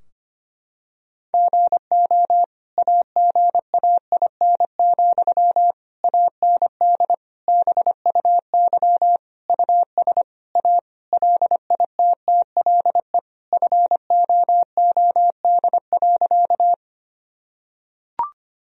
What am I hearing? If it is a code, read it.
Morse code, heard as a keyed tone, about 25 wpm, 700 Hz. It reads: GO AGAIN, AND BUY US A LITTLE FOOD.
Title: Sentences from Top 1000 Words - Review Part 4 - 25wpm